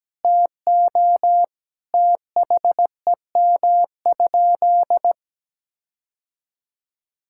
Morse code, heard as a keyed tone, about 17 wpm, 700 Hz. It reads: TO THEM?
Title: Sentences from Top 1600 Words 17wpm